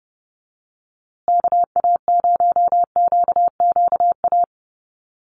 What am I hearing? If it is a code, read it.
Morse code, heard as a keyed tone, about 30 wpm, 700 Hz. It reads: KA0QQA
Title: USA Call Signs Repeated 3x 30wpm